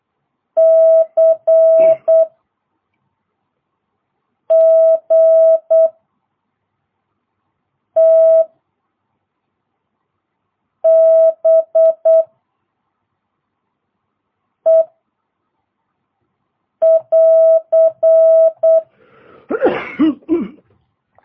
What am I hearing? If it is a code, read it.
Morse code, heard as a keyed tone, about 8 wpm, 650 Hz. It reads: C G T B E <AR>